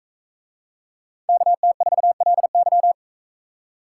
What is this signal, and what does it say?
Morse code, heard as a keyed tone, about 44 wpm, 700 Hz. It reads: KT4LY